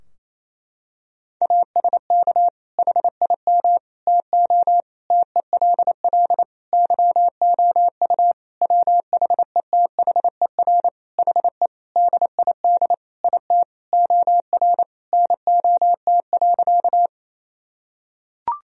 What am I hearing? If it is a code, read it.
Morse code, heard as a keyed tone, about 28 wpm, 700 Hz. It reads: ASK HIM TO TELL YOU WHETHER HE DID IT OR NOT.
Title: Sentences from Top 1000 Words - Review Part 3 - 28wpm